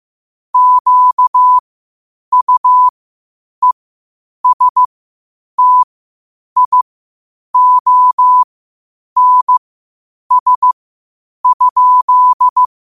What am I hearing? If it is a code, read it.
Morse code, heard as a keyed tone, about 15 wpm, spaced as for 8 wpm, 1000 Hz. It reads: QUESTIONS?
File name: qotdfiles / 8wpm